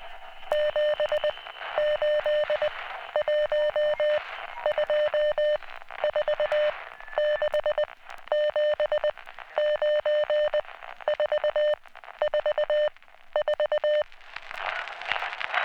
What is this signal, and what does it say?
Morse code, heard as a keyed tone, about 20 wpm, 600 Hz. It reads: 7 8 1 2 4 6 7 9 4 4 4